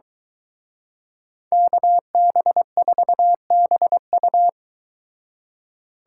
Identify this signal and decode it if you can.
Morse code, heard as a keyed tone, about 23 wpm, 700 Hz. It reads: KB4BU